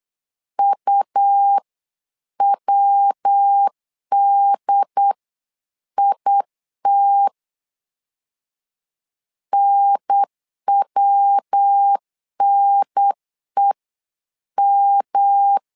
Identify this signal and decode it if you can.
Morse code, heard as a keyed tone, about 8 wpm, spaced as for 7 wpm, 800 Hz. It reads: UWDIT NWNEM
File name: LEZIONI / 3 MWRO / 3 MW 8wpm 2501